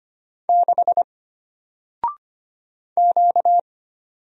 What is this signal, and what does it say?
Morse code, heard as a keyed tone, about 25 wpm, 700 Hz. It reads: 6 Q
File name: Single-Letter-Number-25wpm